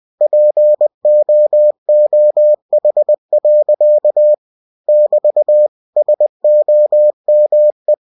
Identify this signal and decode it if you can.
Morse code, heard as a keyed tone, about 20 wpm, 600 Hz. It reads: POOH. <BT> SOME
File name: cw-books / winnie-the-pooh / winnie-20wpm-page-0046